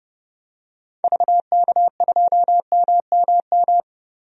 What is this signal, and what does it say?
Morse code, heard as a keyed tone, about 30 wpm, 700 Hz. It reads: VK2MMM
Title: DXCC-100 Call Signs - 6 Characters - 30wpm